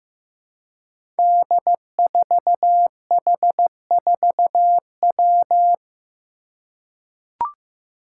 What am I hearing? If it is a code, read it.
Morse code, heard as a keyed tone, about 15 wpm, 700 Hz. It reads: D4H4W
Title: Five Challenging Letter-Number Combinations 15wpm